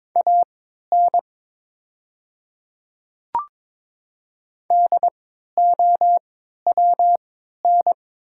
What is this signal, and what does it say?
Morse code, heard as a keyed tone, about 22 wpm, spaced as for 12 wpm, 700 Hz. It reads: AN DOWN
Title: 3x Character Spacing - Top 100 Words - 22wpm